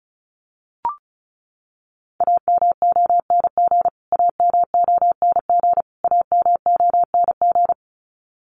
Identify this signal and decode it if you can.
Morse code, heard as a keyed tone, about 34 wpm, 700 Hz. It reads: AMONG AMONG AMONG
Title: Top 500 Words Repeated 35wpm